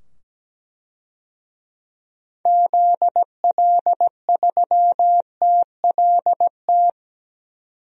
Morse code, heard as a keyed tone, about 17 wpm, 700 Hz. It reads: ZL3TLT
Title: DXCC-100 Call Signs - 6 Characters - 17wpm